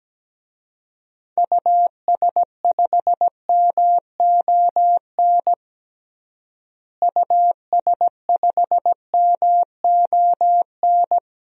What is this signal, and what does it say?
Morse code, heard as a keyed tone, about 17 wpm, 700 Hz. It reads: US5MON US5MON
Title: DXCC-100 Call Signs - 6 Characters - Repeated 3x - 17wpm